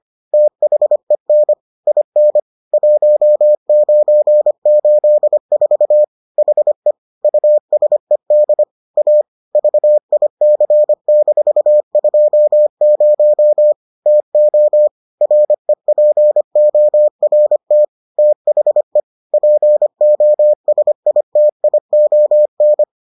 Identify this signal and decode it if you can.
Morse code, heard as a keyed tone, about 25 wpm, 600 Hz. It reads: THEN IN 1984 HE USED A VIC-20 TO REPORT THE POSITION